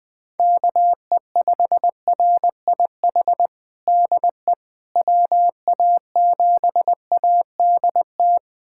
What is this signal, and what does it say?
Morse code, heard as a keyed tone, about 20 wpm, 700 Hz. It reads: KE5RIH DE WA7ADT
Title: USA Callsign DE Callsign - 6 Characters - 20wpm